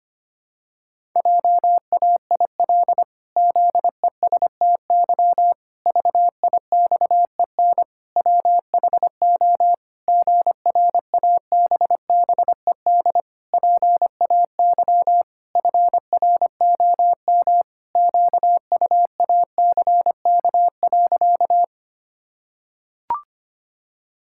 Morse code, heard as a keyed tone, about 25 wpm, 700 Hz. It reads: JAIL ZESTY VIXEN WHO GRABBED PAY FROM QUACK.
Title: Pangram Challenge v2 - 25wpm